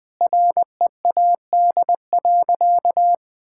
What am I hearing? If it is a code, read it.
Morse code, heard as a keyed tone, about 20 wpm, 700 Hz. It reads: READ.
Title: Sentences from Top 1000 Words - Review Part 5 - 20wpm